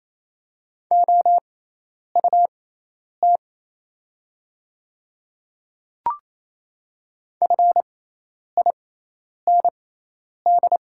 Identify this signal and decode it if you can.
Morse code, heard as a keyed tone, about 27 wpm, spaced as for 10 wpm, 700 Hz. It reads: OUT FIND
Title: 6x Character Spacing - Top 100 Words - 28wpm